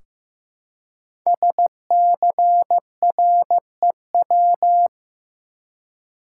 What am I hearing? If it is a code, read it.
Morse code, heard as a keyed tone, about 15 wpm, 700 Hz. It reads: SCREW
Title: Five Letter Words - 15wpm